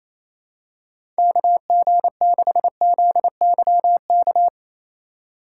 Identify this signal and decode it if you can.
Morse code, heard as a keyed tone, about 28 wpm, 700 Hz. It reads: KG6ZYK
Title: USA Call Signs - 6 Characters - 3x Character Spacing - 28wpm